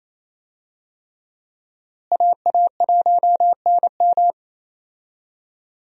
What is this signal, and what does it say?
Morse code, heard as a keyed tone, about 27 wpm, 700 Hz. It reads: AA1NM
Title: USA Call Signs - 5 Characters - Repeated 3x - 28wpm